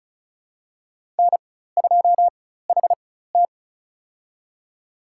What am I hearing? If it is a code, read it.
Morse code, heard as a keyed tone, about 34 wpm, spaced as for 16 wpm, 700 Hz. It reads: N2HT
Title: USA Call Signs - 4 Characters - 4x Character Spacing - 35wpm